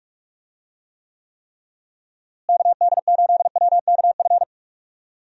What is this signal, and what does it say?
Morse code, heard as a keyed tone, about 46 wpm, 700 Hz. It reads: KD8WKF
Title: USA Call Signs - 6 Characters - Repeated 3x - 45wpm